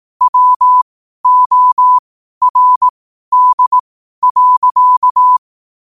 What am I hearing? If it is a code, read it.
Morse code, heard as a keyed tone, about 18 wpm, spaced as for 12 wpm, 1000 Hz. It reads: WORD.
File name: qotdfiles / 12wpm